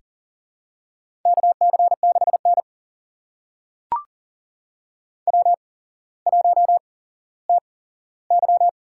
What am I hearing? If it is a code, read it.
Morse code, heard as a keyed tone, about 38 wpm, spaced as for 17 wpm, 700 Hz. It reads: KC6N W1TY
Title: USA Call Signs - 4 Characters - 8x Character Spacing - 40wpm